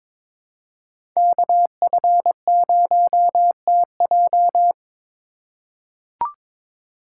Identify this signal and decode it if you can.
Morse code, heard as a keyed tone, about 21 wpm, 700 Hz. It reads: KF0TJ
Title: USA Call Signs - 5 Characters - 4x Character Spacing - 22wpm